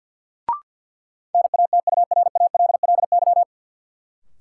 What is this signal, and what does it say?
Morse code, heard as a keyed tone, about 53 wpm, 700 Hz. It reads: NATURALLY
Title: Sentences from Top 1300 Words - New Word Drill Alt - 50wpm